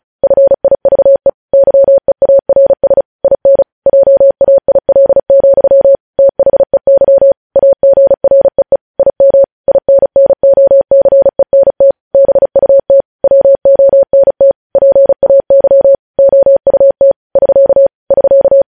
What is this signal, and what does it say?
Morse code, heard as a keyed tone, about 36 wpm, 550 Hz. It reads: FIVE YEARS IN JAIL, THEY AGREE IM INNOCENT BUT WONT PAY OUT <SK> <SK>
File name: podcast / News headlines-1x-35x35